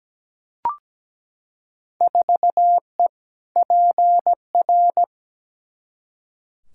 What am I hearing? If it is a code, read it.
Morse code, heard as a keyed tone, about 17 wpm, 700 Hz. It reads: 4E PR
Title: Field Day Exchanges Encore - 17wpm